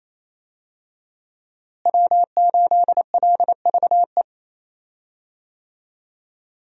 Morse code, heard as a keyed tone, about 28 wpm, 700 Hz. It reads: W8LVE
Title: USA Call Signs - 5 Characters - Repeated 3x - 28wpm